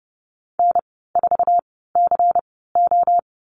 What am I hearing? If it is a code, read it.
Morse code, heard as a keyed tone, about 30 wpm, spaced as for 17 wpm, 700 Hz. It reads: N4CO
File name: Lesson 19g - Letter w - Call Signs 30wpm